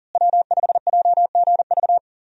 Morse code, heard as a keyed tone, about 40 wpm, 700 Hz. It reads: W5JGV
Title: USA Call Signs - 5 Characters - Repeated 2x - 40wpm